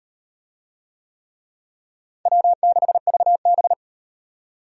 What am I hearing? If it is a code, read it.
Morse code, heard as a keyed tone, about 36 wpm, 700 Hz. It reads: W6VB